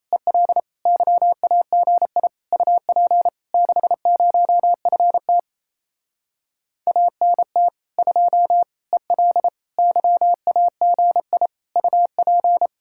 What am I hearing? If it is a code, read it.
Morse code, heard as a keyed tone, about 30 wpm, 700 Hz. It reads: EL YAGI UP 60FT ANT 2 EL YAGI UP
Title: Speed-Racing Common QSO Phrases - 28wpm